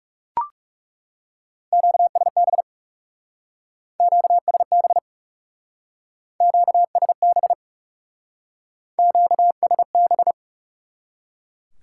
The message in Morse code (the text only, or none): QSB QSB QSB QSB